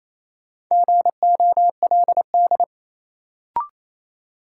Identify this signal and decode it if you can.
Morse code, heard as a keyed tone, about 28 wpm, 700 Hz. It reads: GOLD
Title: Sentences from Top 1000 Words - Review Part 3 - New Word Drill - 28wpm